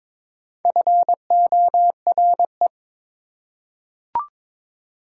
Morse code, heard as a keyed tone, about 22 wpm, 700 Hz. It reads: FORE
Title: Prefixes and Suffixes 22wpm